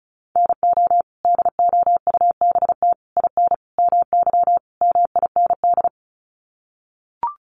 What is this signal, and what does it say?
Morse code, heard as a keyed tone, about 34 wpm, 700 Hz. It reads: NO DOUBT IN MY MIND
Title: Sets of 5 Words Encore2 35wpm